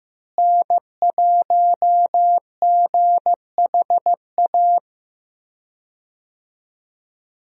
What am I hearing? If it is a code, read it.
Morse code, heard as a keyed tone, about 16 wpm, 700 Hz. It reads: N1GHA
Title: USA Call Signs - 5 Characters - Repeated 2x - 15wpm